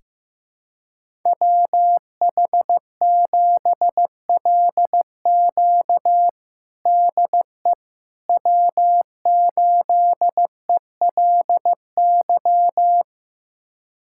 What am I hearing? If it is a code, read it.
Morse code, heard as a keyed tone, about 15 wpm, 700 Hz. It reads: WH7LQ DE W8ELY
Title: USA Callsign DE Callsign - 5 Characters - 15wpm